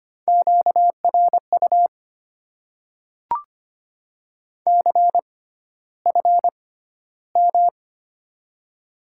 Morse code, heard as a keyed tone, about 25 wpm, spaced as for 13 wpm, 690 Hz. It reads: QRU CFM
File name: Top 90 QSO Elements - 6x Character Spacing - 25wpm